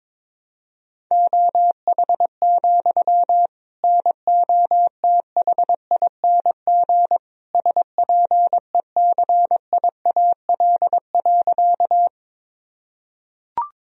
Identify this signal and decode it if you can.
Morse code, heard as a keyed tone, about 22 wpm, 700 Hz. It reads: OH, NOTHING SPECIAL.
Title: Sentences from Top 1000 Words - Review Part 3 - 22wpm